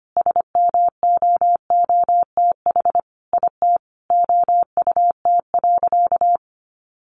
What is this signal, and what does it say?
Morse code, heard as a keyed tone, about 25 wpm, 700 Hz. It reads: SMOOTH IT OUT.
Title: Sentences from Top 1400 Words 25wpm